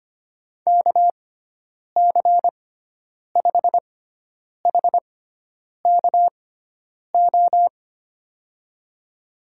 Morse code, heard as a keyed tone, about 25 wpm, spaced as for 9 wpm, 700 Hz. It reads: KC5HKO
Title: USA Call Signs - 6 Characters - 6x Character Spacing - 25wpm